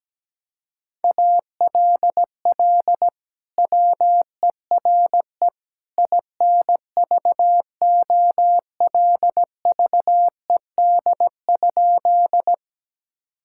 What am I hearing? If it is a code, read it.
Morse code, heard as a keyed tone, about 17 wpm, 700 Hz. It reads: ALL WERE INVOLVED?
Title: Sentences from Top 1000 Words - Review Part 5 - 17wpm